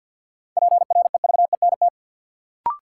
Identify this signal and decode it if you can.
Morse code, heard as a keyed tone, about 52 wpm, 700 Hz. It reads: PREVENT
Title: Sentences from Top 1100 Words - New Word Drill Alt - 50wpm